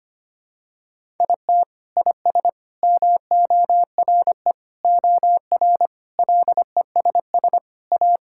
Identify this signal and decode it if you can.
Morse code, heard as a keyed tone, about 25 wpm, 700 Hz. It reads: IT IS MORE OR LESS A
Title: Sentences from Top 1000 Words - Review Part 6 - 25wpm